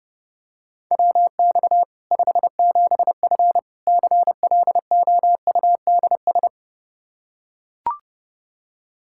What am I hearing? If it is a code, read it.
Morse code, heard as a keyed tone, about 32 wpm, 700 Hz. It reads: WX 57F CLOUDS
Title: K1USN SST QSO Practice 30wpm